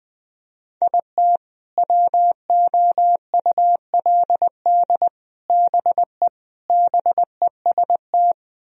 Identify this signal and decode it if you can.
Morse code, heard as a keyed tone, about 21 wpm, 700 Hz. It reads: IT WOULD BE BEST
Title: Sentences from Top 1000 Words - Review Part 1 - 20wpm